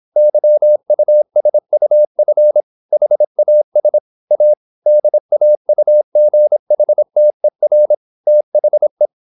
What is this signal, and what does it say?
Morse code, heard as a keyed tone, about 26 wpm, 600 Hz. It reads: YUSUF HAS A DAUGHTER THE
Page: Never 26wpm | CW med Gnister